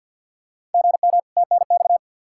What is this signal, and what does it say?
Morse code, heard as a keyed tone, about 46 wpm, 700 Hz. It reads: GM TNX
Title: Practice QSO - Exchange 1 - Beginning - 50wpm